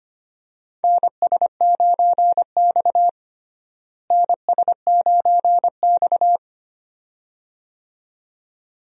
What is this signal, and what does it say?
Morse code, heard as a keyed tone, about 25 wpm, 700 Hz. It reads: NS9X NS9X
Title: USA Call Signs - 4 Characters - Repeated 2x - 25wpm